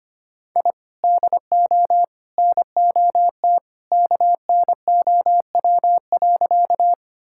Morse code, heard as a keyed tone, about 25 wpm, 700 Hz. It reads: I DO NOT KNOW.
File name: practice / 25wpm / Sentences with 3-Word Phrases - 25wpm